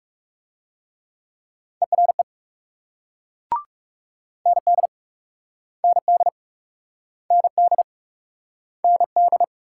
Morse code, heard as a keyed tone, about 36 wpm, 700 Hz. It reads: ERE ND ND ND ND
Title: Speed-Racing Multi-letter Phonemes - 30wpm